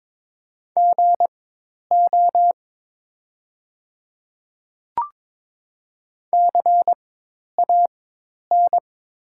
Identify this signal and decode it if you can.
Morse code, heard as a keyed tone, about 22 wpm, spaced as for 10 wpm, 700 Hz. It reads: GO CAN
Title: 4x Character Spacing - Top 100 Words - 22wpm